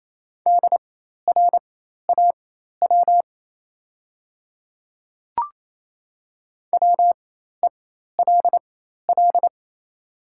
Morse code, heard as a keyed tone, about 28 wpm, spaced as for 13 wpm, 690 Hz. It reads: DRAW WELL
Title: 4x Character Spacing - Top 500 Words - 28wpm